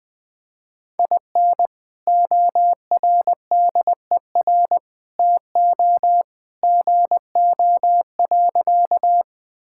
Morse code, heard as a keyed tone, about 20 wpm, 700 Hz. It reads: IN ORDER TO GO.
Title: Sentences with 2-Word Phrases - Spoken Beginning - 20wpm